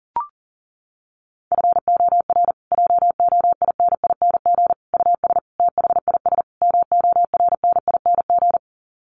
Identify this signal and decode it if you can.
Morse code, heard as a keyed tone, about 40 wpm, 700 Hz. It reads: FOR JOINING US THIS MORNING
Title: Sets of 5 Words Encore 40wpm